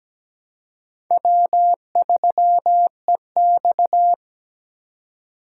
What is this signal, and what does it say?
Morse code, heard as a keyed tone, about 17 wpm, 700 Hz. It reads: W3EX
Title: USA Call Signs - 3x Character Spacing - 17wpm